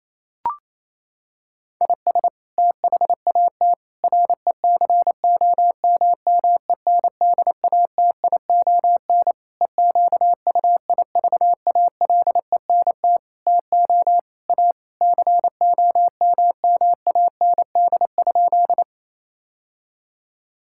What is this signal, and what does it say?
Morse code, heard as a keyed tone, about 28 wpm, 700 Hz. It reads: IS THAT RECOMMENDATION EQUIVALENT TO A COMMAND?
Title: Sentences from Top 1600 Words 28wpm